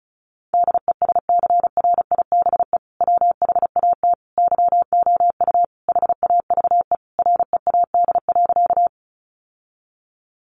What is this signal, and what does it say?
Morse code, heard as a keyed tone, about 34 wpm, 700 Hz. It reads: DESCRIBE WHAT YOU HAVE READ.
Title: Sentences from Top 1000 Words 35wpm